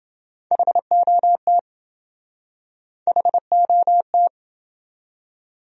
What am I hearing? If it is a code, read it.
Morse code, heard as a keyed tone, about 27 wpm, 700 Hz. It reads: HOT HOT